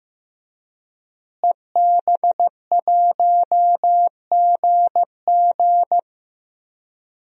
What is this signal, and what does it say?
Morse code, heard as a keyed tone, about 15 wpm, 700 Hz. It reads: EB1GG
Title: DXCC-100 Call Signs - 5 Characters - Repeated 2x - 15wpm